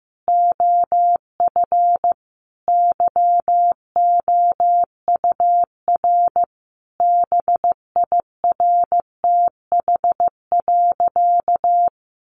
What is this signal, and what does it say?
Morse code, heard as a keyed tone, about 14 wpm, 700 Hz. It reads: OF YOUR BIRTH.
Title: Sentences from Top 1300 Words 15wpm